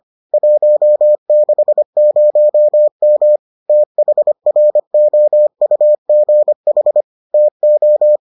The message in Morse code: 160M THROUGH TO

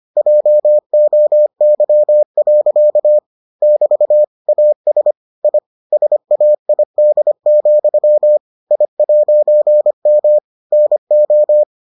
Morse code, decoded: JOY. <BT> AS I SAID, I'M NO

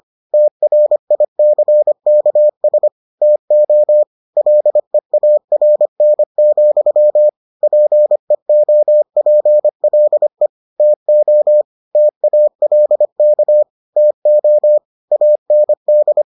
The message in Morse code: TRICKS TO LEARN, PEOPLE TO TALK TO AND

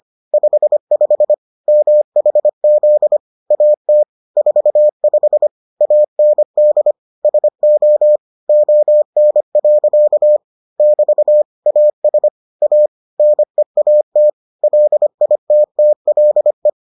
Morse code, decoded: 55 MHZ AT 45 AND SO ON. <BT> AS A NEAT LITTLE

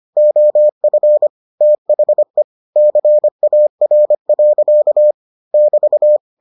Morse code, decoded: OF THE CAR. <BT>

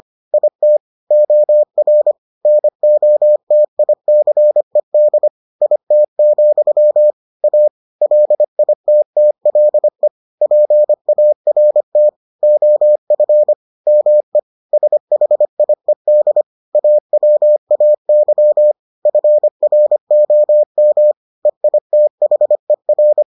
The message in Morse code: IT OR NOTICED IT, A LITTLE PART OF ME SHIED AWAY FROM EITHER

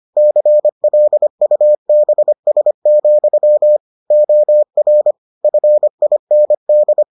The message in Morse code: CLUBS, OR FIND

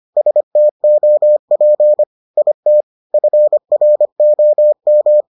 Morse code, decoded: STOP IT FROM

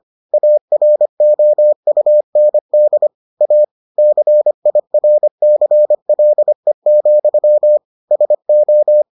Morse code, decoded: AROUND A CIRCLE, SO

